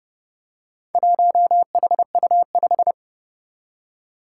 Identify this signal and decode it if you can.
Morse code, heard as a keyed tone, about 30 wpm, 690 Hz. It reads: 1HU5
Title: Four Challenging Letter-Number Combinations 30wpm